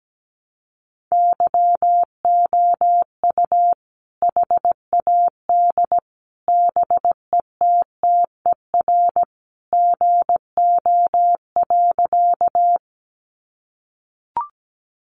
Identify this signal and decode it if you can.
Morse code, heard as a keyed tone, about 17 wpm, 700 Hz. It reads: YOU HAD BETTER GO.
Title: Sentences from Top 400 Words 17wpm